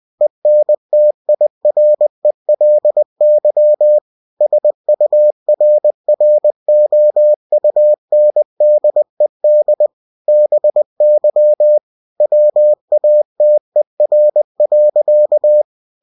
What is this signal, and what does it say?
Morse code, heard as a keyed tone, about 20 wpm, 600 Hz. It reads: ENTIRELY SURROUNDED BY WATER.
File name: cw-books / winnie-the-pooh / winnie-20wpm-page-0091